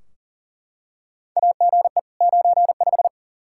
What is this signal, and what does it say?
Morse code, heard as a keyed tone, about 40 wpm, 700 Hz. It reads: AGE 95
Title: K1USN SST QSO Practice 40wpm